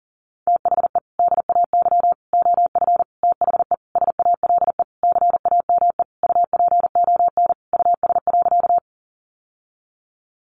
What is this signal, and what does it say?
Morse code, heard as a keyed tone, about 40 wpm, 700 Hz. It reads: THE DAY OF THE SALE CAME UPON US.
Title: Sentences from Top 1400 Words 40wpm